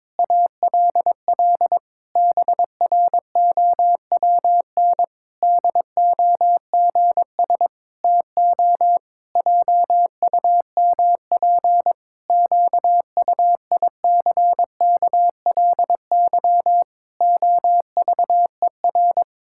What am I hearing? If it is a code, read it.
Morse code, heard as a keyed tone, about 22 wpm, 700 Hz. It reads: ALL BROWN DOGS TO JUMP QUICKLY OVER
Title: Pangram Challenge v2 - 22wpm